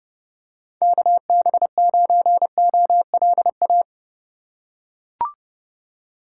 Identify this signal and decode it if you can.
Morse code, heard as a keyed tone, about 31 wpm, 700 Hz. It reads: KB9OLA
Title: USA Call Signs - 4x Character Spacing - 30wpm